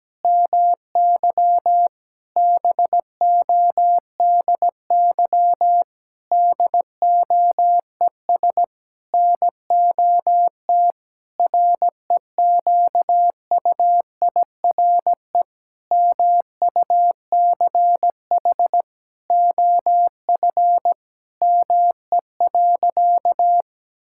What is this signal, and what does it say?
Morse code, heard as a keyed tone, about 17 wpm, 700 Hz. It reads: MY BODY DOES NOT REQUIRE MUCH OF ME.
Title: Sentences from Top 1000 Words - Review Part 6 - 17wpm